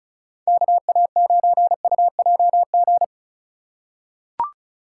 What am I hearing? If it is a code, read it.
Morse code, heard as a keyed tone, about 34 wpm, 700 Hz. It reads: KA9UJG
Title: USA Call Signs - 6 Characters - 2x Character Spacing - 35wpm